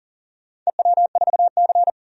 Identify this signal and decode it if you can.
Morse code, heard as a keyed tone, about 40 wpm, 700 Hz. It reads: EW4C
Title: DXCC-100 Call Signs - 4 Characters - 40wpm